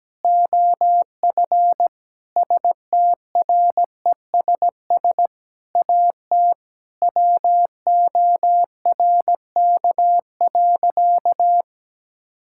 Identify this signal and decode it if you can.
Morse code, heard as a keyed tone, about 17 wpm, 700 Hz. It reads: OF STRESS AT WORK.
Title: Sentences from Top 1000 Words - Review Part 6 - 17wpm